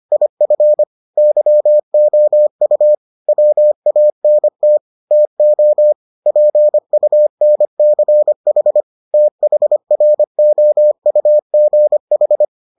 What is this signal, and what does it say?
Morse code, heard as a keyed tone, about 25 wpm, 600 Hz. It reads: IF YOU WANT TO PUNCH THROUGH